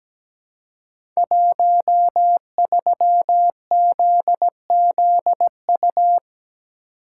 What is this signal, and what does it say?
Morse code, heard as a keyed tone, about 17 wpm, 700 Hz. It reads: 13ZZU
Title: Code Groups - 5 Letter:Numbers - 17wpm